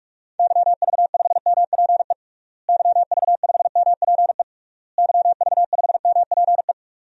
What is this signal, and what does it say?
Morse code, heard as a keyed tone, about 46 wpm, 700 Hz. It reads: YV5MPE YV5MPE YV5MPE
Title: DXCC-100 Call Signs - 6 Characters - Repeated 3x - 45wpm